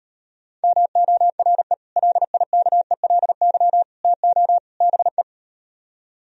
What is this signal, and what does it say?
Morse code, heard as a keyed tone, about 38 wpm, 700 Hz. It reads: MORE LIKELY TO BE